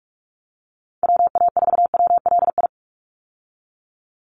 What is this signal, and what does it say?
Morse code, heard as a keyed tone, about 44 wpm, 700 Hz. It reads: WA4WLI
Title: Call Signs Repeated 2x - 45wpm